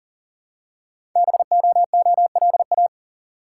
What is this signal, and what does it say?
Morse code, heard as a keyed tone, about 39 wpm, 700 Hz. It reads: BOOLA
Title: Code Groups - 5 Letters - 40wpm